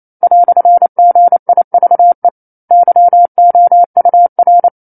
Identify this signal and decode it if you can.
Morse code, heard as a keyed tone, about 28 wpm, 700 Hz. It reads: "GIVE YOUR
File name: 09-Chapter-09 28 WPM